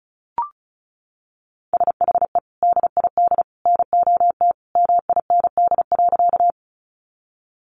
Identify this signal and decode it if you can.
Morse code, heard as a keyed tone, about 34 wpm, 700 Hz. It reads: SHE DID NOT MIND.
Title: Sentences from Top 600 Words 35wpm